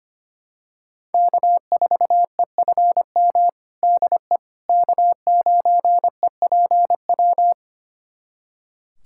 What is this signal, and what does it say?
Morse code, heard as a keyed tone, about 25 wpm, 700 Hz. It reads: K4EFM DE K9EPW
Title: USA Callsign DE Callsign - 5 Characters - 25wpm